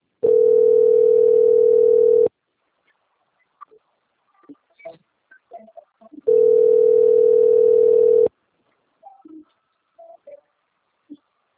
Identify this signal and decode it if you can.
Ringback tone (North America), 440 + 480 Hz.
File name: out- - - - - .27080